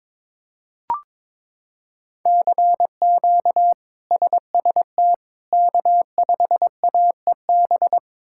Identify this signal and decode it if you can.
Morse code, heard as a keyed tone, about 23 wpm, 700 Hz. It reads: CQ SST K5AEB
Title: K1USN SST QSO Practice - 22wpm